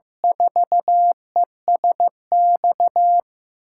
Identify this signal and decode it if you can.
Morse code, heard as a keyed tone, about 15 wpm, 700 Hz. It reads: 4ESX
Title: USA Call Signs - 5 Characters - 6x Character Spacing - 15wpm